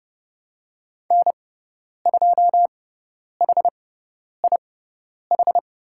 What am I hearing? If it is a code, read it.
Morse code, heard as a keyed tone, about 29 wpm, spaced as for 10 wpm, 700 Hz. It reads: N2HIH
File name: Lesson 17g - Letter u - Call Signs 30wpm